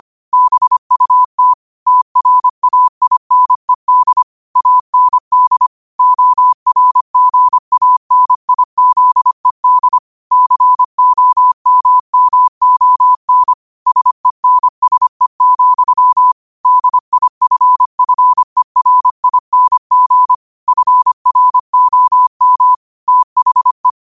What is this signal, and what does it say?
Morse code, heard as a keyed tone, about 25 wpm, 1000 Hz. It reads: BUT TRAINED AND ORGANIZED COMMON SENSE, DIFFERING FROM THE
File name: qotdfiles / 25wpm